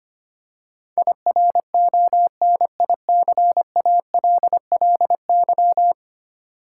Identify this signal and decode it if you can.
Morse code, heard as a keyed tone, about 25 wpm, 700 Hz. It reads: IRONICALLY
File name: 10 Letter Words - First 3 Letters Given - 25wpm